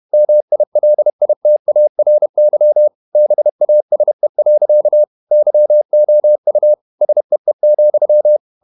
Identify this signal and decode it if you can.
Morse code, heard as a keyed tone, about 31 wpm, 600 Hz. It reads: MILITARY BASE. YOU SEE,
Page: Never 31wpm | CW med Gnister